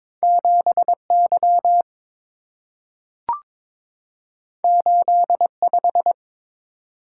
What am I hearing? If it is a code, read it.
Morse code, heard as a keyed tone, about 23 wpm, 700 Hz. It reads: 7Y 85
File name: practice / 22wpm / Two Letter-Number Combinations - 22wpm